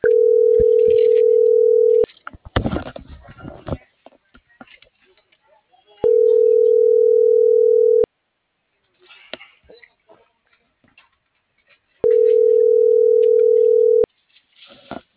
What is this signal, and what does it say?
Ringback tone (North America), 440 + 480 Hz.